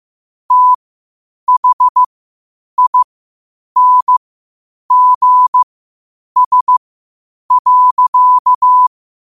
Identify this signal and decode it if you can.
Morse code, heard as a keyed tone, about 15 wpm, spaced as for 8 wpm, 1000 Hz. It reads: THINGS.
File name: 8wpm